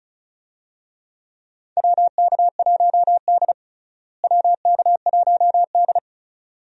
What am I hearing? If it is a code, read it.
Morse code, heard as a keyed tone, about 34 wpm, 700 Hz. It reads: WK1D WK1D
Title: USA Call Signs - 4 Characters - Repeated 2x - 35wpm